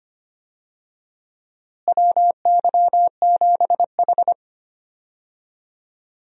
Morse code, heard as a keyed tone, about 25 wpm, 700 Hz. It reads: WY7H
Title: USA Call Signs - 4 Characters - Repeated 3x - 25wpm